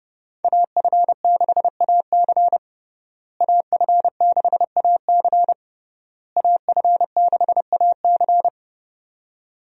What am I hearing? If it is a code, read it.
Morse code, heard as a keyed tone, about 32 wpm, 700 Hz. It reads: AF6AC AF6AC AF6AC
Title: USA Call Signs - 5 Characters - Repeated 3x - 30wpm